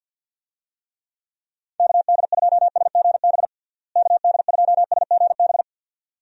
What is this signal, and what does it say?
Morse code, heard as a keyed tone, about 50 wpm, 700 Hz. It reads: KD2SGB KD2SGB
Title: USA Call Signs - 6 Characters - Repeated 2x - 50wpm